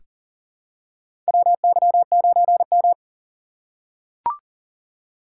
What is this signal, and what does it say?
Morse code, heard as a keyed tone, about 40 wpm, 700 Hz. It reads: WY9M
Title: USA Call Signs - 6x Character Spacing - 40wpm